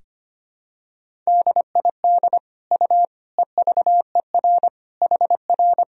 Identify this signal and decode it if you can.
Morse code, heard as a keyed tone, about 25 wpm, 700 Hz. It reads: DID U EVER HR
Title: Sentences from Top 400 Words and Abbreviations 25wpm